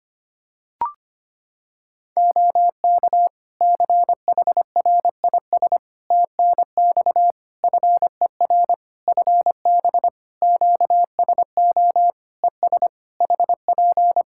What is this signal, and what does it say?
Morse code, heard as a keyed tone, about 25 wpm, 700 Hz. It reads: OK CHRIS TNX FER FB QSO ES HP
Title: Practice QSO - Exchange 4 - Ending - 25wpm